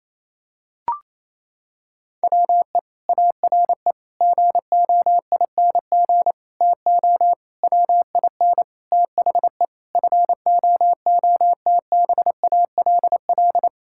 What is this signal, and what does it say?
Morse code, heard as a keyed tone, about 28 wpm, 700 Hz. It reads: WE ARE GOING TO WIN THE FOOTBALL
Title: Sentences from Top 1600 Words 28wpm